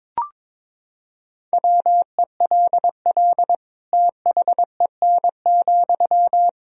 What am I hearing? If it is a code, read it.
Morse code, heard as a keyed tone, about 22 wpm, 700 Hz. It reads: WELL THEN,
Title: Sentences from Top 200 Words and Abbreviations 22wpm